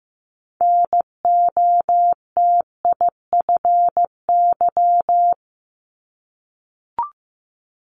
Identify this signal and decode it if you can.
Morse code, heard as a keyed tone, about 15 wpm, 700 Hz. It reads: NOTIFY
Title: Multi-letter Phonemes 15wpm